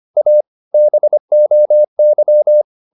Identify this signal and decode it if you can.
Morse code, heard as a keyed tone, about 25 wpm, 600 Hz. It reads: A BOY
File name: the old man and the sea part 1 25wpm